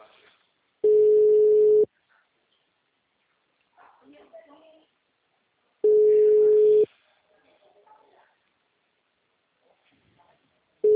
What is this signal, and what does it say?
Ringback tone (Europe, 425 Hz), with speech.